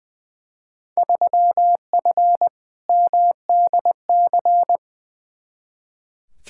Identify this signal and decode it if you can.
Morse code, heard as a keyed tone, about 20 wpm, 700 Hz. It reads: 3F MDC
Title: Field Day Exchanges Encore - 20wpm